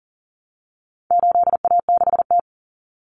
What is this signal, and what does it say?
Morse code, heard as a keyed tone, about 40 wpm, 700 Hz. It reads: 8A6T
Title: Common Cut Numbers - 4-Digit 40wpm